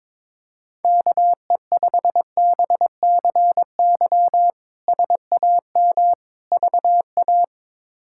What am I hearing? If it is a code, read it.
Morse code, heard as a keyed tone, about 22 wpm, 700 Hz. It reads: KE5BCY SAM VA
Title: K1USN SST QSO Practice - 22wpm